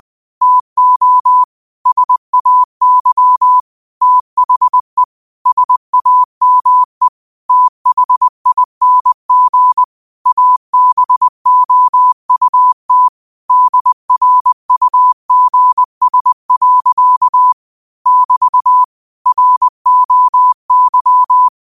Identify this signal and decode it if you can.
Morse code, heard as a keyed tone, about 20 wpm, 1000 Hz. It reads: TO SAY THE SAME THING ABOUT DRUGS. <BT> ROY
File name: qotdfiles / 20wpm